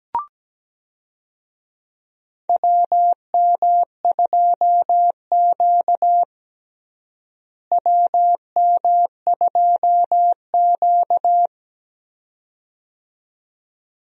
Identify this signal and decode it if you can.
Morse code, heard as a keyed tone, about 17 wpm, 700 Hz. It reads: WM2Q WM2Q
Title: USA Call Signs - 4 Characters - Repeated 2x - 17wpm